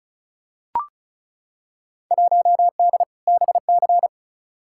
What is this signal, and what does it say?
Morse code, heard as a keyed tone, about 36 wpm, 700 Hz. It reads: 1D BC
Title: Field Day Exchanges - 35wpm